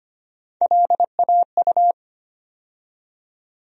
Morse code, heard as a keyed tone, about 25 wpm, 700 Hz. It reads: LAU
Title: USA Call Sign Postfixes - 3 Letters 25wpm